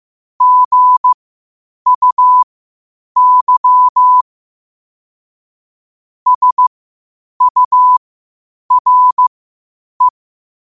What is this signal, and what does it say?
Morse code, heard as a keyed tone, about 15 wpm, spaced as for 8 wpm, 1000 Hz. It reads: GUY SURE
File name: qotdfiles / 8wpm